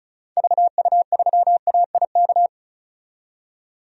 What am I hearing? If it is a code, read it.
Morse code, heard as a keyed tone, about 34 wpm, 700 Hz. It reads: VU3AIK
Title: DXCC-100 Call Signs - 6 Characters - 35wpm